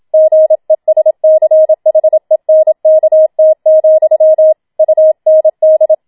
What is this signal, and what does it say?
Morse code, heard as a keyed tone, about 27 wpm, 630 Hz. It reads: GESCHENKT, UND
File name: Schneewittchen Teil2 27wpm